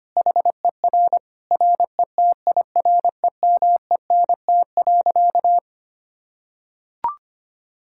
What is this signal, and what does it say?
Morse code, heard as a keyed tone, about 25 wpm, 700 Hz. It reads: HER RETIREMENT.
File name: practice / 25wpm / Difficult Sentences with Hints - 25wpm